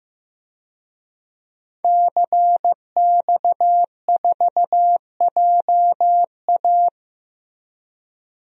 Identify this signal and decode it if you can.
Morse code, heard as a keyed tone, about 15 wpm, 700 Hz. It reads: CX4JA
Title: DXCC-100 Call Signs - 5 Characters - Repeated 2x - 15wpm